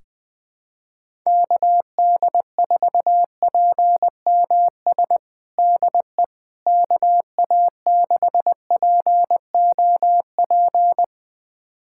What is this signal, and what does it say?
Morse code, heard as a keyed tone, about 20 wpm, 700 Hz. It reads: KD4PMS DE KA6POP
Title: USA Callsign DE Callsign 20wpm